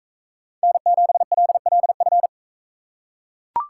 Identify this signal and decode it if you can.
Morse code, heard as a keyed tone, about 39 wpm, 700 Hz. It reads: N7LLF